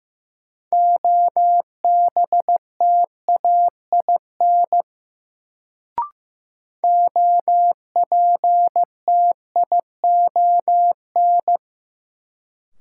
Morse code, heard as a keyed tone, about 15 wpm, 700 Hz. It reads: OBTAIN OPTION
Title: Sentences from Top 1300 Words - New Word Drill Alt - 15wpm